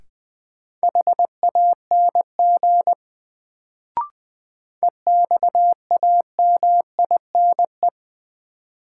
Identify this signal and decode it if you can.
Morse code, heard as a keyed tone, about 20 wpm, 700 Hz. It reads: HANG EXAMINE
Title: Sentences from Top 1600 Words - New Word Drill - 20wpm